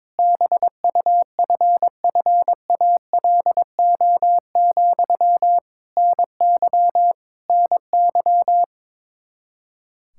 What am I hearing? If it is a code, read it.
Morse code, heard as a keyed tone, about 22 wpm, 700 Hz. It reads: BUFFALO, NY NY
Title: K1USN SST QSO Practice 22wpm